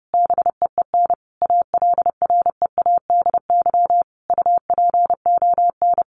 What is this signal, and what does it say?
Morse code, heard as a keyed tone, about 30 wpm, 700 Hz. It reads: BEEN ALREADY UPON
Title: Sentences from Top 1500 Words 30wpm